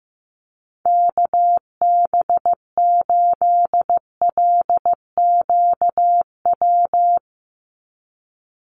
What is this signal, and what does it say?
Morse code, heard as a keyed tone, about 15 wpm, 700 Hz. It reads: KB8LQW
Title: USA Call Signs 15wpm